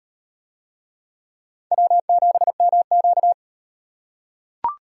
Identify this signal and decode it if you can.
Morse code, heard as a keyed tone, about 36 wpm, 700 Hz. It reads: W7MQ